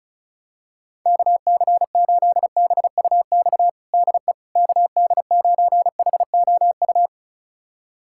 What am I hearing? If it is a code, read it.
Morse code, heard as a keyed tone, about 36 wpm, 700 Hz. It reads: KC8BUX DE KD9HOU
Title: USA Callsign DE Callsign - 6 Characters - 35wpm